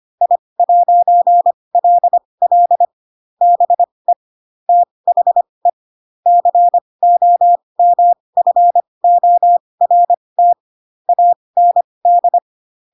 Morse code, heard as a keyed tone, about 25 wpm, spaced as for 20 wpm, 700 Hz. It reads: I'LL BE THE COMFORT AND
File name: Pinocchio / Pinocchio ch8 15 25 ik1hge